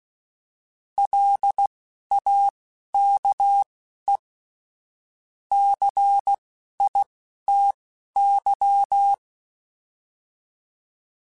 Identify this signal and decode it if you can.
Morse code, heard as a keyed tone, about 16 wpm, spaced as for 11 wpm, 800 Hz. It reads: LAKE CITY